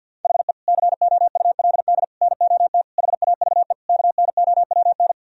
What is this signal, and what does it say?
Morse code, heard as a keyed tone, about 50 wpm, 700 Hz. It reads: HE COULD NOT HAVE KNOWN
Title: Sentences from Top 1000 Words - Review Part 3 - 50wpm